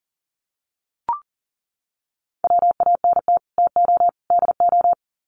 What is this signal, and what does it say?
Morse code, heard as a keyed tone, about 40 wpm, 700 Hz. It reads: WANT TO DO
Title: Sets of 3 Words 40wpm